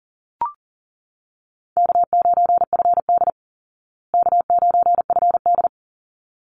Morse code, heard as a keyed tone, about 40 wpm, 700 Hz. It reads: K9FD K9FD
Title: Easy Call Signs Repeated 2x - 40wpm